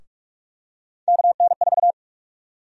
Morse code, heard as a keyed tone, about 44 wpm, 700 Hz. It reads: KN4
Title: USA Call Sign Prefixes - 45wpm